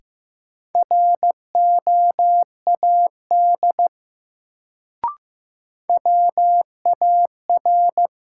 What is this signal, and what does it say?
Morse code, heard as a keyed tone, about 15 wpm, 700 Hz. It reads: ROAD WAR
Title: Sets of 5 Words - New Word Drill - 15wpm